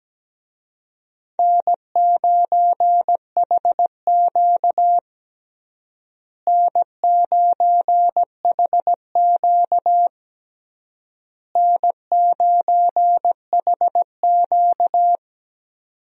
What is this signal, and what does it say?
Morse code, heard as a keyed tone, about 17 wpm, 700 Hz. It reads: N9HQ N9HQ N9HQ
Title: USA Call Signs - 4 Characters - Repeated 3x - 17wpm